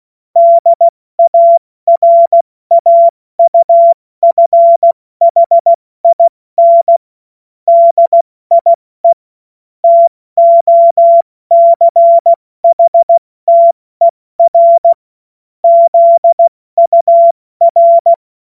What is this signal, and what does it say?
Morse code, heard as a keyed tone, about 16 wpm, spaced as for 14 wpm, 680 Hz. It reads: DARAUFHIN DIE TOCHTER ZUR
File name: Morsekurs / Witze / Witz 31. Januar 2023 70 F80